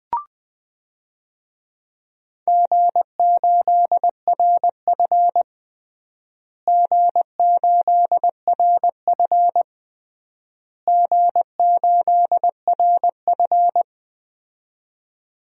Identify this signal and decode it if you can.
Morse code, heard as a keyed tone, about 20 wpm, 700 Hz. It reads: G8RF G8RF G8RF
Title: DXCC-100 Call Signs - 4 Characters - Repeated 3x - 20wpm